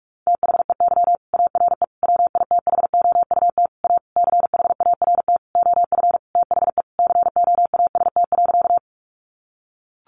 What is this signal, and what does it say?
Morse code, heard as a keyed tone, about 44 wpm, 700 Hz. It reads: THEY ARE WITHOUT A CHART OF THE COAST.
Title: Sentences from Top 1300 Words 45wpm